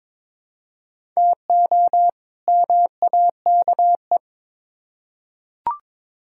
Morse code, heard as a keyed tone, about 22 wpm, 700 Hz. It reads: TO MAKE
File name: Build A Phrase - 2-Word Phrases - 22wpm